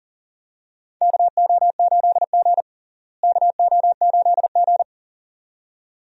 Morse code, heard as a keyed tone, about 38 wpm, 700 Hz. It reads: KO8G KO8G
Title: USA Call Signs - 4 Characters - Repeated 2x - 40wpm